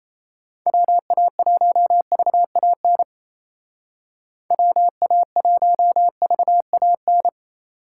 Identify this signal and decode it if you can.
Morse code, heard as a keyed tone, about 35 wpm, 700 Hz. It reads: WA1VAN WA1VAN
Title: Speed Racing - USA 2x3 Call Signs 28wpm